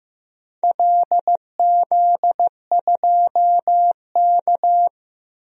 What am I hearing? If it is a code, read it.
Morse code, heard as a keyed tone, about 15 wpm, 700 Hz. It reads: LZ2K
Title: DXCC-100 Call Signs - 4 Characters - 15wpm